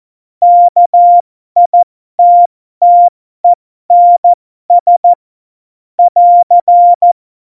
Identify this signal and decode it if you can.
Morse code, heard as a keyed tone, about 14 wpm, spaced as for 12 wpm, 700 Hz. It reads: KITTENS <AR>
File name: Morsekurs / Witze